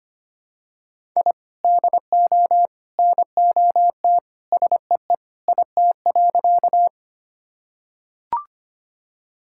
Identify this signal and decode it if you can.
Morse code, heard as a keyed tone, about 25 wpm, 700 Hz. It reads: I DO NOT SEE IT.
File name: practice / 25wpm / Sentences with 3-Word Phrases - 25wpm